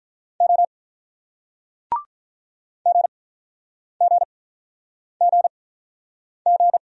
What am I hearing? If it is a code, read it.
Morse code, heard as a keyed tone, about 48 wpm, spaced as for 9 wpm, 700 Hz. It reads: O GGGG